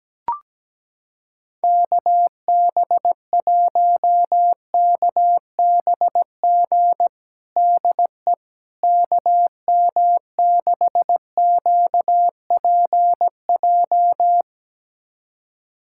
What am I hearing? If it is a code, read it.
Morse code, heard as a keyed tone, about 17 wpm, 700 Hz. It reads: KB1KBG DE KM6QPJ
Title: USA Callsign DE Callsign 17wpm